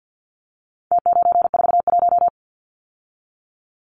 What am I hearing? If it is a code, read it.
Morse code, heard as a keyed tone, about 48 wpm, 700 Hz. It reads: T941
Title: Numbers - Contest Style - 4-Digit 50wpm